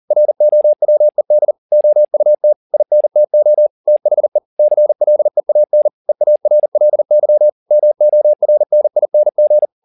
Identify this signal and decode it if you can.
Morse code, heard as a keyed tone, about 40 wpm, 600 Hz. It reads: ROWED OUT INTO THE CLEAN EARLY MORNING